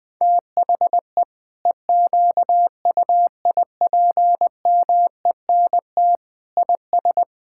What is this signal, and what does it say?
Morse code, heard as a keyed tone, about 20 wpm, 700 Hz. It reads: THE EQUIPMENT IS
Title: Sentences from Top 1000 Words - Review Part 5 - 20wpm